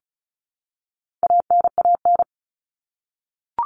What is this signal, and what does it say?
Morse code, heard as a keyed tone, about 34 wpm, 700 Hz. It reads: ANAN
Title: All Cut Numbers - 4-Digit 35wpm